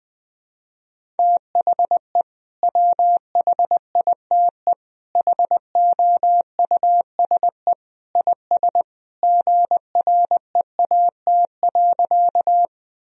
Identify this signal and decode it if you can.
Morse code, heard as a keyed tone, about 21 wpm, 700 Hz. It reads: THE WHITE HOUSE IS GREAT.
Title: Sentences from Top 1000 Words - Review Part 1 - 20wpm